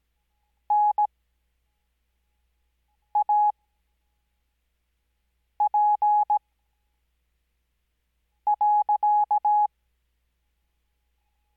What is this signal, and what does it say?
Morse code, heard as a keyed tone, about 17 wpm, spaced as for 4 wpm, 840 Hz. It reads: NAP.
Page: Farnsworth Spacing: 15WPM Character speed Spaced at 5WPM.